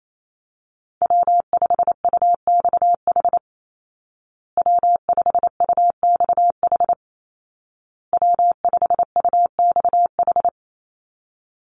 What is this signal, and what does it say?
Morse code, heard as a keyed tone, about 28 wpm, 700 Hz. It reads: W5UXH W5UXH W5UXH
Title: Call Signs Repeated 2x - 28wpm